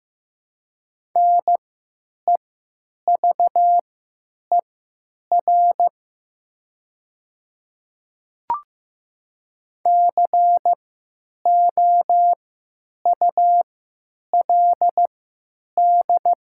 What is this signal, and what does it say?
Morse code, heard as a keyed tone, about 15 wpm, spaced as for 9 wpm, 700 Hz. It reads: NEVER COULD
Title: 3x Character Spacing - Top 300 Words - 15wpm